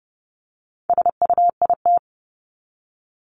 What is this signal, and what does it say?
Morse code, heard as a keyed tone, about 30 wpm, 700 Hz. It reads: SUIT
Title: Top 1501-1750 Words 30wpm